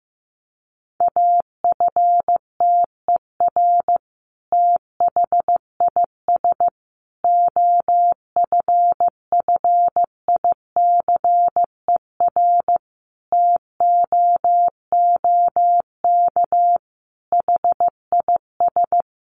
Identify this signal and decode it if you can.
Morse code, heard as a keyed tone, about 14 wpm, 700 Hz. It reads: AFTER THIS OFFICER TOOK HIS
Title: Sentences from Top 1200 Words 15wpm